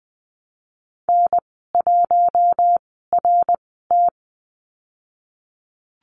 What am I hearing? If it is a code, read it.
Morse code, heard as a keyed tone, about 20 wpm, spaced as for 14 wpm, 700 Hz. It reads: N1RT
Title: 088 Lesson 15g - Number 5 - Call Signs 20wpm